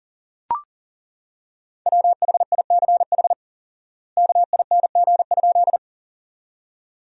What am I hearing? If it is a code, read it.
Morse code, heard as a keyed tone, about 40 wpm, 700 Hz. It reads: WHICH KING?
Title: 4x Word-Spacing - Sentences from Top 400 Words - 40wpm